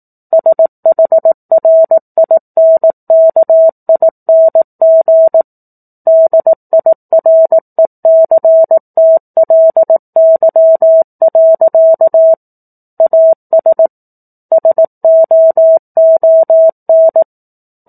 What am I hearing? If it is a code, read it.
Morse code, heard as a keyed tone, about 18 wpm, 650 Hz. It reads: SHRINKING DIRECTLY. AS SOON
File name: 04-Chapter-04 18 WPM